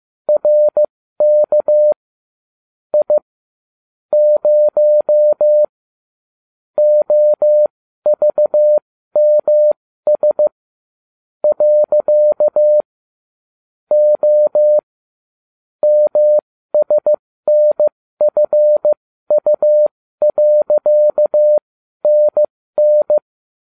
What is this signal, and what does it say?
Morse code, heard as a keyed tone, about 15 wpm, spaced as for 13 wpm, 600 Hz. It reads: RK I 0 OVMS . O MSNFU.NN